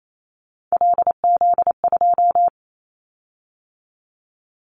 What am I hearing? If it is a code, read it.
Morse code, heard as a keyed tone, about 28 wpm, 700 Hz. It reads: LZ2
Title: Callsign Prefixes for Top 25 DXCC - 28wpm